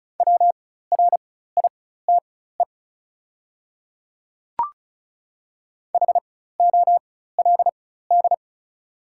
Morse code, heard as a keyed tone, about 36 wpm, spaced as for 16 wpm, 690 Hz. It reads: WRITE HOLD
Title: 4x Character Spacing - Top 500 Words - 35wpm